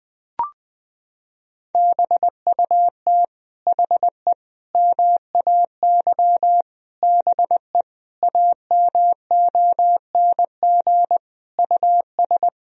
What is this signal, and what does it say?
Morse code, heard as a keyed tone, about 21 wpm, 700 Hz. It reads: BUT HE MAY BE AMONG US
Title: Sentences from Top 1000 Words - Review Part 3 - 20wpm